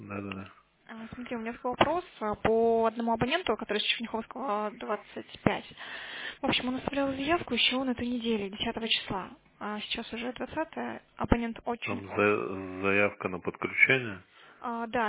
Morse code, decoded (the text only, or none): none